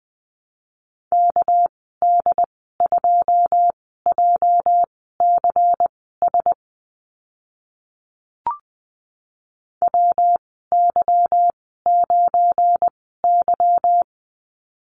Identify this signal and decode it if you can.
Morse code, heard as a keyed tone, about 20 wpm, spaced as for 14 wpm, 700 Hz. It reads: KD2JCS WY9Y
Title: 231 Lesson 37g - Letter x - Call Signs 20wpm